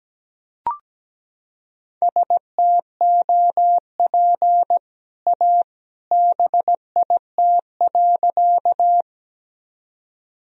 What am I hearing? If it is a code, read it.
Morse code, heard as a keyed tone, about 17 wpm, 700 Hz. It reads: STOP A BIT.
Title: Sentences from Top 1000 Words - Review Part 1 - 17wpm